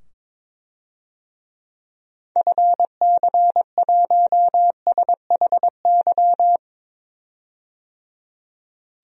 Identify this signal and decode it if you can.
Morse code, heard as a keyed tone, about 22 wpm, 700 Hz. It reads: FC1SHY
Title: DXCC-100 Call Signs - 6 Characters - 22wpm